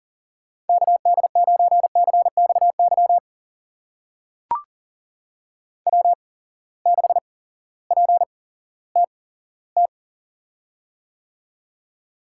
Morse code, heard as a keyed tone, about 40 wpm, spaced as for 18 wpm, 700 Hz. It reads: KD9CXY W6PTT
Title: USA Call Signs - 8x Character Spacing - 40wpm